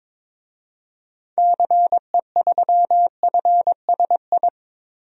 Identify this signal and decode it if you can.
Morse code, heard as a keyed tone, about 22 wpm, 700 Hz. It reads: CE3FSI
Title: DXCC-100 Call Signs - 6 Characters - Repeated 2x - 22wpm